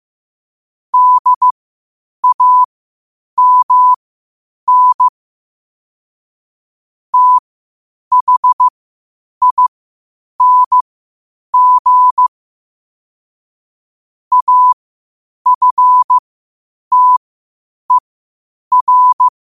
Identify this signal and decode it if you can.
Morse code, heard as a keyed tone, about 15 wpm, spaced as for 8 wpm, 1000 Hz. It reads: DAMN THING AFTER
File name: qotdfiles / 8wpm